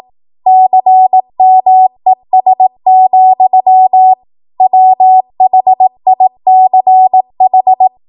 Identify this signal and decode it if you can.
Morse code, heard as a keyed tone, about 18 wpm, 750 Hz. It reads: CMES, WHICH
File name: Morse / 18WPM